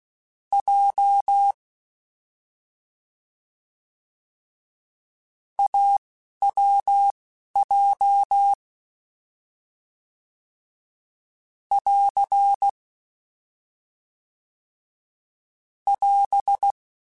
Morse code, heard as a keyed tone, about 16 wpm, spaced as for 11 wpm, 800 Hz. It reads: J AWJ <AR> <AS>
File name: Difficult Combos / Difficult Combos-mp3s / AWJ ShortRun @10